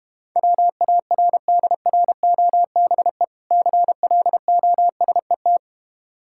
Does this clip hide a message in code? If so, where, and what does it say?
Yes: Morse code from 0.4 to 5.6 s, heard as a keyed tone, about 32 wpm, 700 Hz. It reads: WARDROBE CLOSET